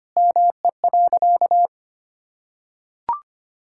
Morse code, heard as a keyed tone, about 25 wpm, 700 Hz. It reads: ME.
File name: practice / 25wpm / Sentences with 2-Word Phrases - 25wpm